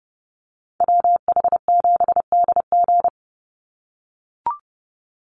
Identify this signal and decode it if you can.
Morse code, heard as a keyed tone, about 30 wpm, 700 Hz. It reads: WH7DG
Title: USA Call Signs 30wpm